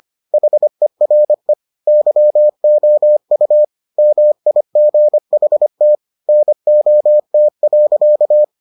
Morse code, heard as a keyed tone, about 25 wpm, 600 Hz. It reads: HERE YOU MIGHT NOT.